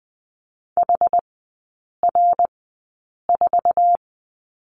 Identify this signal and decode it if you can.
Morse code, heard as a keyed tone, about 20 wpm, spaced as for 8 wpm, 700 Hz. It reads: HR4
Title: 081 Lesson 14e - Number 2 - Three Characters 20wpm